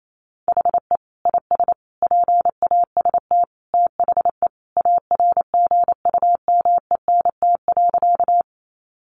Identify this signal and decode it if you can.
Morse code, heard as a keyed tone, about 28 wpm, 700 Hz. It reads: HE IS PAST THE ARGUMENT.
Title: Sentences from Top 1400 Words 28wpm